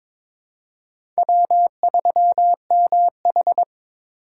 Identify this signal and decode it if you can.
Morse code, heard as a keyed tone, about 22 wpm, 690 Hz. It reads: W3MH
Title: USA Call Signs - 4 Characters - 3x Character Spacing - 22wpm